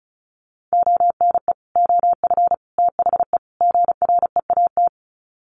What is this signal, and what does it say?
Morse code, heard as a keyed tone, about 34 wpm, 700 Hz. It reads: ONE OF THE GREAT
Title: Sets of 4 Words 35wpm